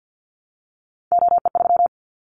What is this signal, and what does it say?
Morse code, heard as a keyed tone, about 50 wpm, 700 Hz. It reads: OE3
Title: Callsign Prefixes for Top 50 DXCC - 50wpm